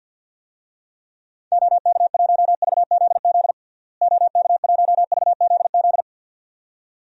Morse code, heard as a keyed tone, about 52 wpm, 700 Hz. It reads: OK1VZZ OK1VZZ
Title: DXCC-100 Call Signs - 6 Characters - Repeated 2x - 50wpm